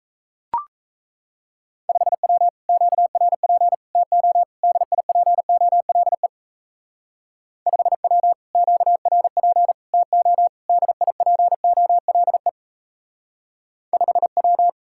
Morse code, heard as a keyed tone, about 33 wpm, 700 Hz. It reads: 5W QRP TO DIPOLE 5W QRP TO DIPOLE 5W
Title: Speed-Racing Common QSO Phrases - 28wpm